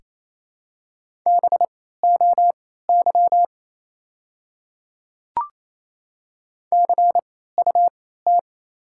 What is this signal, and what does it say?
Morse code, heard as a keyed tone, about 28 wpm, spaced as for 16 wpm, 700 Hz. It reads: BOY CUT
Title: 3x Character Spacing - Top 500 Words - 28wpm